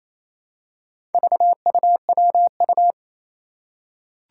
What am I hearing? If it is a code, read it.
Morse code, heard as a keyed tone, about 27 wpm, 700 Hz. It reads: VUWU
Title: Four Challenging Letter-Number Combinations 28wpm